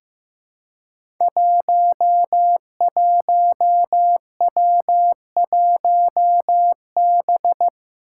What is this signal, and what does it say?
Morse code, heard as a keyed tone, about 15 wpm, 700 Hz. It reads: 11W1B
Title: Five Challenging Letter-Number Combinations 15wpm